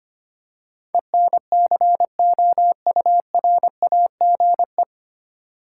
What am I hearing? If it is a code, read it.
Morse code, heard as a keyed tone, about 25 wpm, 700 Hz. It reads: ENCOURAGE
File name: Nine Letter Words - 25wpm